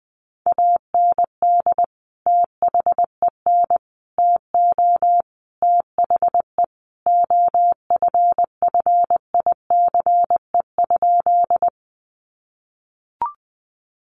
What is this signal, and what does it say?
Morse code, heard as a keyed tone, about 20 wpm, 700 Hz. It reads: AND THEN TO THE OFFICE?
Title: Sentences from Top 1000 Words 20wpm